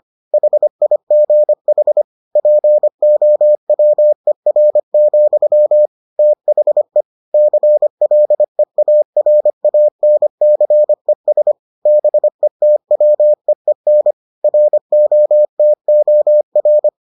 Morse code, heard as a keyed tone, about 25 wpm, 600 Hz. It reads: HIGH POWER, THE CLEARANCES BETWEEN ROTOR